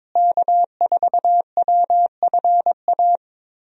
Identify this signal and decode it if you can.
Morse code, heard as a keyed tone, about 22 wpm, 700 Hz. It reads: K4WFA
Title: USA Call Signs - 5 Characters - Repeated 2x - 22wpm